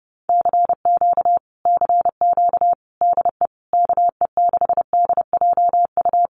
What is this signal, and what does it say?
Morse code, heard as a keyed tone, about 30 wpm, 700 Hz. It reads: CQ CQ DE KE6DJU
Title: Common QSO Phrases Encore 30wpm